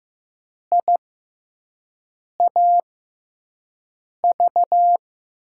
Morse code, heard as a keyed tone, about 16 wpm, spaced as for 5 wpm, 700 Hz. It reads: IAV
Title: USA Call Signs - 6 Characters - 6x Character Spacing - 15wpm